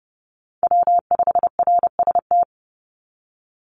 Morse code, heard as a keyed tone, about 30 wpm, 700 Hz. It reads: W5RST
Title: NA SOTA Chaser Callsigns 30wpm